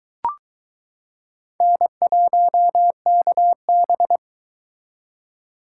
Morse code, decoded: N1KB

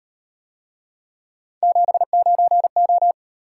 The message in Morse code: 79O

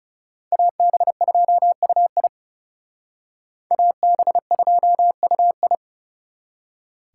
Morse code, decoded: AB2UI AB2UI

